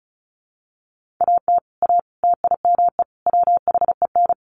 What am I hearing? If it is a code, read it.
Morse code, heard as a keyed tone, about 34 wpm, 700 Hz. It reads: AT A TIME WHEN
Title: Sets of 4 Words 35wpm